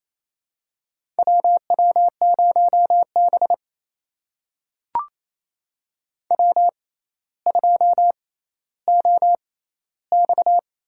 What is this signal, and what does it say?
Morse code, heard as a keyed tone, about 28 wpm, spaced as for 14 wpm, 700 Hz. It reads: WW0B W2OX
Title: USA Call Signs - 4 Characters - 6x Character Spacing - 28wpm